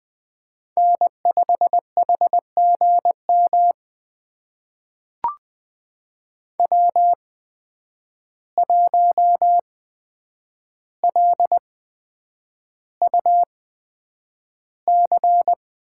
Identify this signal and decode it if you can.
Morse code, heard as a keyed tone, about 20 wpm, spaced as for 9 wpm, 700 Hz. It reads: N5HGM W1LUC
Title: USA Call Signs - 5 Characters - 8x Character Spacing - 20wpm